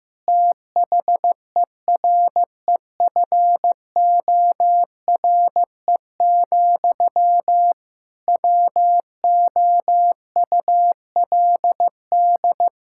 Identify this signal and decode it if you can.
Morse code, heard as a keyed tone, about 15 wpm, 700 Hz. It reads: THEREFORE, WOULD
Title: Sentences from Top 1000 Words - Review Part 5 - 15wpm